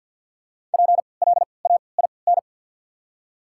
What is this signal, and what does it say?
Morse code, heard as a keyed tone, about 54 wpm, spaced as for 29 wpm, 700 Hz. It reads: PLAIN
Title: 3x Character Spacing - Top 500 Words - 50wpm